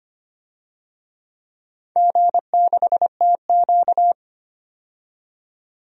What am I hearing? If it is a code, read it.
Morse code, heard as a keyed tone, about 25 wpm, 700 Hz. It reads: G6TQ
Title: DXCC-100 Call Signs - 4 Characters - 25wpm